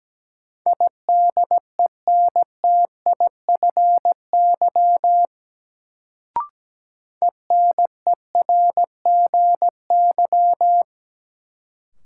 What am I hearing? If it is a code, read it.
Morse code, heard as a keyed tone, about 17 wpm, 700 Hz. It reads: IDENTIFY ENERGY
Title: Sentences from Top 900 Words - New Word Drill - 17wpm